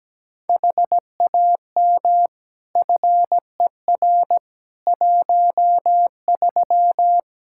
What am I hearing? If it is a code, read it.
Morse code, heard as a keyed tone, about 17 wpm, 700 Hz. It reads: HAM FER 13
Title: Practice QSO - Exchange 3 - More Info - 17wpm